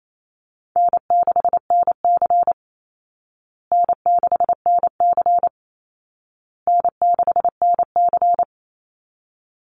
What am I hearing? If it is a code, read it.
Morse code, heard as a keyed tone, about 28 wpm, 700 Hz. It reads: N6NC N6NC N6NC
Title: Easy Call Signs Repeated 3x - 28wpm